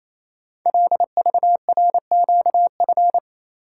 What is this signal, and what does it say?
Morse code, heard as a keyed tone, about 27 wpm, 700 Hz. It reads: LVRQF
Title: Code Groups - 5 Letters - 28wpm